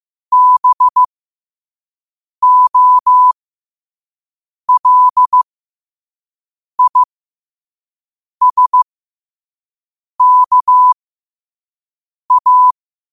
Morse code, heard as a keyed tone, about 15 wpm, spaced as for 5 wpm, 1000 Hz. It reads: BOLISKA
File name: qotdfiles / 5wpm